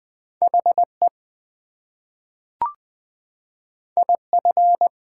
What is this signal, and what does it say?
Morse code, heard as a keyed tone, about 20 wpm, 700 Hz. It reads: HE IF
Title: Two Letter Words - 20wpm